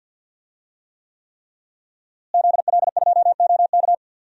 Morse code, heard as a keyed tone, about 46 wpm, 700 Hz. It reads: ZL2OK
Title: DXCC-100 Call Signs - 5 Characters - Repeated 3x - 50wpm